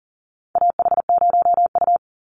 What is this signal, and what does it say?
Morse code, heard as a keyed tone, about 40 wpm, 700 Hz. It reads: AH0U
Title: Call Signs - 40wpm